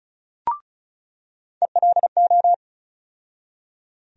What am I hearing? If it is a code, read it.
Morse code, heard as a keyed tone, about 34 wpm, 700 Hz. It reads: ELO
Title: USA Call Sign Postfixes - 3 Letters 35wpm